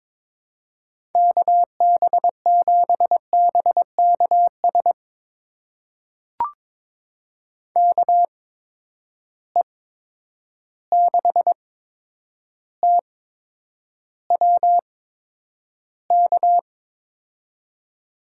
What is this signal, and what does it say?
Morse code, heard as a keyed tone, about 22 wpm, spaced as for 9 wpm, 700 Hz. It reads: KB7BKS KE6TWK
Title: USA Call Signs - 8x Character Spacing - 22wpm